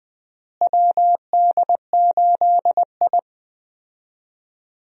Morse code, heard as a keyed tone, about 20 wpm, 700 Hz. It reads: WD8I
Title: USA Call Signs - 4 Characters - Repeated 2x - 20wpm